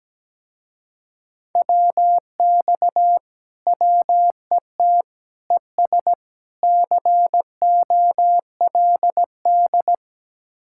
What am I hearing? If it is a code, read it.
Morse code, heard as a keyed tone, about 18 wpm, 700 Hz. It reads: WX WET ES COLD
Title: Speed-Racing Common QSO Phrases - 17wpm